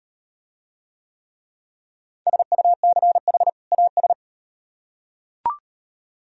Morse code, heard as a keyed tone, about 38 wpm, 700 Hz. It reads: SUCH AS